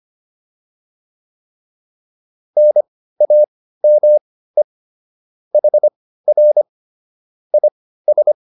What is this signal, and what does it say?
Morse code, heard as a keyed tone, about 25 wpm, spaced as for 15 wpm, 600 Hz. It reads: NAME HR IS